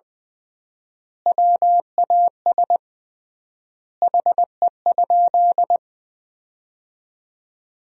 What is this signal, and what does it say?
Morse code, heard as a keyed tone, about 21 wpm, 700 Hz. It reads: WAS HE?
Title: 3x Word-Spacing - Sentences from Top 100 Words - 20wpm